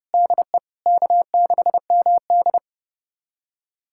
Morse code, heard as a keyed tone, about 30 wpm, 700 Hz. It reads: DE K6MD
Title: USA Callsign DE Callsign - 4 Characters - 30wpm